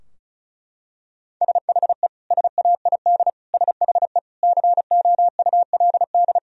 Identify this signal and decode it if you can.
Morse code, heard as a keyed tone, about 36 wpm, 700 Hz. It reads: SHE SAID SHE COULD
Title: Sentences from Top 1000 Words - Review Part 4 - 35wpm